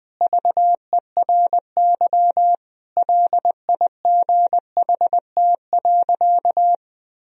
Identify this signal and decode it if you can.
Morse code, heard as a keyed tone, about 20 wpm, 700 Hz. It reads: VERY LIGHT.
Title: Sentences from Top 1000 Words - Review Part 4 - 20wpm